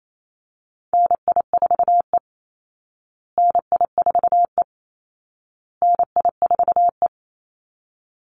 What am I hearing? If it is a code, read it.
Morse code, heard as a keyed tone, about 28 wpm, 700 Hz. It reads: NI4E NI4E NI4E
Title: Call Signs Repeated 2x - 28wpm